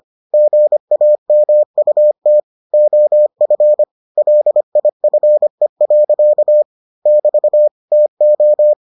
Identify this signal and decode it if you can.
Morse code, heard as a keyed tone, about 25 wpm, 600 Hz. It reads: GAMUT OF LIFE. <BT> TO